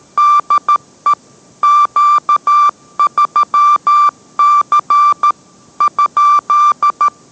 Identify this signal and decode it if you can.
Morse code, heard as a keyed tone, about 13 wpm, 1180 Hz. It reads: DE Q3C ?